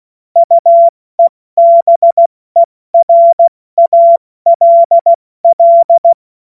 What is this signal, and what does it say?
Morse code, heard as a keyed tone, about 16 wpm, 680 Hz. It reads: UEBERALL
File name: Morsekurs / Witze / Witz 31. Januar 2023 70 F80